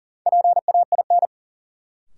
Morse code, heard as a keyed tone, about 40 wpm, 700 Hz. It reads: PAIN
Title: Sentences from Top 1600 Words - New Word Drill - 40wpm